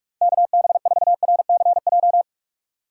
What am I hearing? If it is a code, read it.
Morse code, heard as a keyed tone, about 41 wpm, 700 Hz. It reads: KB4RCJ
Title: USA Call Signs - 6 Characters - 2x Character Spacing - 45wpm